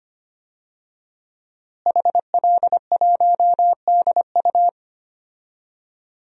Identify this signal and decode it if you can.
Morse code, heard as a keyed tone, about 25 wpm, 700 Hz. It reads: HL1DU
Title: DXCC-100 Call Signs - 5 Characters - Repeated 2x - 25wpm